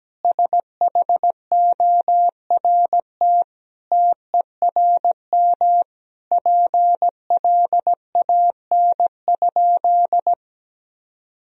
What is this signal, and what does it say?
Morse code, heard as a keyed tone, about 17 wpm, 700 Hz. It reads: SHORT TERM PLAN?
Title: Sentences from Top 1000 Words - Review Part 4 - 17wpm